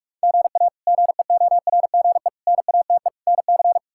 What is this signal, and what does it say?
Morse code, heard as a keyed tone, about 46 wpm, 700 Hz. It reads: GA GEORGE NATE NC
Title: K1USN SST QSO Practice - 45wpm